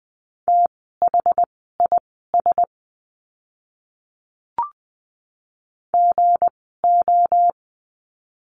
Morse code, heard as a keyed tone, about 20 wpm, spaced as for 14 wpm, 700 Hz. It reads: THIS GO
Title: 244 Lesson 39f - Letter z - Words 20wpm